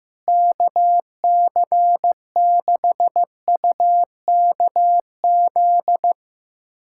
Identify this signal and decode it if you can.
Morse code, heard as a keyed tone, about 15 wpm, 700 Hz. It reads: KC6UKZ
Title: USA Callsign DE Callsign - 6 Characters - 15wpm